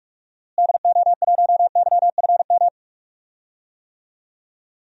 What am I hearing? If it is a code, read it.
Morse code, heard as a keyed tone, about 46 wpm, 700 Hz. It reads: DO1YFM
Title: DXCC-100 Call Signs - 6 Characters - Repeated 2x - 45wpm